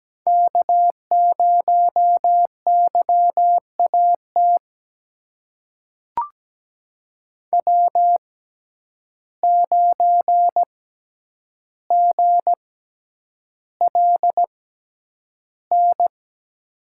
Morse code, heard as a keyed tone, about 17 wpm, spaced as for 9 wpm, 700 Hz. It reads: K0YAT W9GLN
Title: USA Call Signs - 5 Characters - 6x Character Spacing - 17wpm